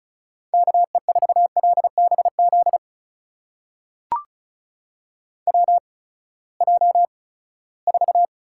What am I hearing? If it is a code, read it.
Morse code, heard as a keyed tone, about 36 wpm, spaced as for 20 wpm, 700 Hz. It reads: KE4LBZ WJ4
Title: USA Call Signs - 8x Character Spacing - 35wpm